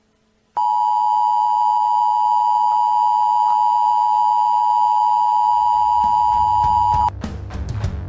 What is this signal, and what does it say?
Emergency Alert System (SAME) tones (the 853 and 960 Hz attention tone). It is heard from 0.5 to 7.1 s.